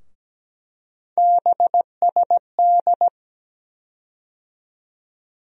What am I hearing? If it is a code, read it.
Morse code, heard as a keyed tone, about 17 wpm, 700 Hz. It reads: BSD
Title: USA Call Sign Postfixes - 3 Letters 17wpm